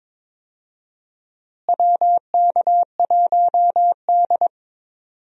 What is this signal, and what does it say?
Morse code, heard as a keyed tone, about 22 wpm, 700 Hz. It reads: WK1D
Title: USA Call Signs - 4 Characters - Repeated 2x - 22wpm